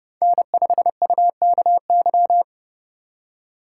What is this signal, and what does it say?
Morse code, heard as a keyed tone, about 31 wpm, 700 Hz. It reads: N5UKY